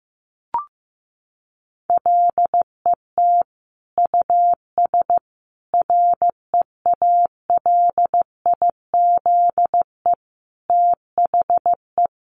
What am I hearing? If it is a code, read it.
Morse code, heard as a keyed tone, about 15 wpm, 700 Hz. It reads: LET US REALIZE THE
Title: Sentences from Top 1300 Words 15wpm